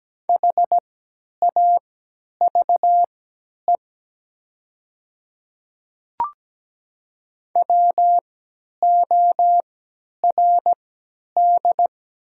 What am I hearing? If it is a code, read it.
Morse code, heard as a keyed tone, about 17 wpm, spaced as for 10 wpm, 700 Hz. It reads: HAVE WORD
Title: 3x Character Spacing - Top 100 Words - 17wpm